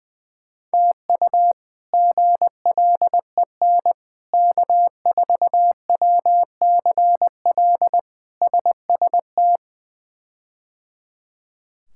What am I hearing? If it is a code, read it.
Morse code, heard as a keyed tone, about 20 wpm, 700 Hz. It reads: TU GLEN K4WCL SST
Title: K1USN SST QSO Practice - 20wpm